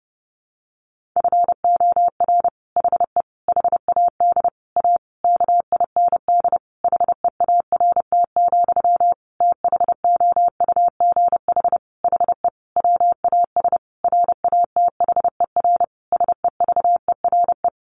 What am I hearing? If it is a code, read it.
Morse code, heard as a keyed tone, about 30 wpm, 700 Hz. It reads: FOR HE HAD A KIND HEART, THOUGH HE WAS RATHER SEVERE
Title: Sentences from Top 1400 Words 30wpm